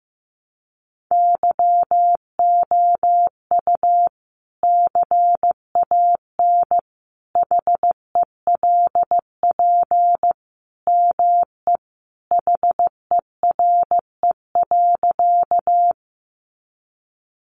Morse code, heard as a keyed tone, about 14 wpm, 700 Hz. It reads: YOU CAN HELP ME HERE.
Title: Sentences from Top 200 Words 15wpm